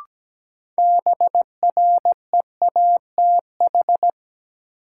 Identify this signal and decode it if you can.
Morse code, heard as a keyed tone, about 17 wpm, 700 Hz. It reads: BREATH
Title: Sets of 5 Words Encore - New Word Drill - 17wpm